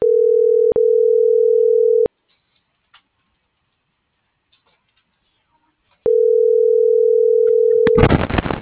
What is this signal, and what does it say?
Ringback tone (North America), 440 + 480 Hz.